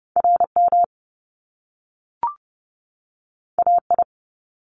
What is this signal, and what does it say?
Morse code, heard as a keyed tone, about 30 wpm, 700 Hz. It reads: R M A I
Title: Top Two Letter Combinations in Words 30wpm